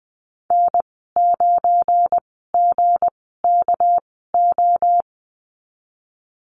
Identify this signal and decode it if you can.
Morse code, heard as a keyed tone, about 20 wpm, spaced as for 15 wpm, 700 Hz. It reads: N9GKO
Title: 245 Lesson 39g - Letter z - Call Signs 20wpm